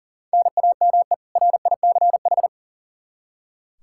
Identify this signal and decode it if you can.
Morse code, heard as a keyed tone, about 37 wpm, 700 Hz. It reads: NAME RICH
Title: K1USN SST QSO Practice 40wpm